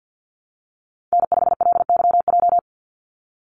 Morse code, heard as a keyed tone, about 48 wpm, 700 Hz. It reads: N5LYJ
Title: Call Signs Repeated 3x - 50wpm